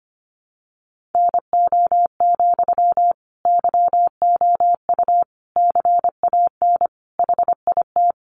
Morse code, heard as a keyed tone, about 25 wpm, 700 Hz. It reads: NO, YOU CAN HIT
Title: Sentences from Top 1200 Words 25wpm